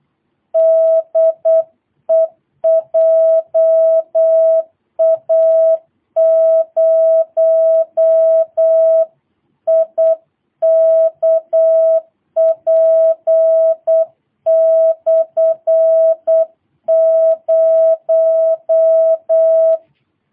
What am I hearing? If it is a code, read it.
Morse code, heard as a keyed tone, about 8 wpm, 650 Hz. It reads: DE JA0IKP/0
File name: jh8cbh / 2022cw